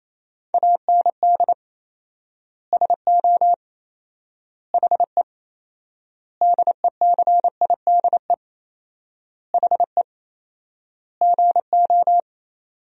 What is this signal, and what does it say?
Morse code, heard as a keyed tone, about 28 wpm, 700 Hz. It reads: AND SO HE DECIDE HE GO
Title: 4x Word-Spacing - Sentences from Top 500 Words - 28wpm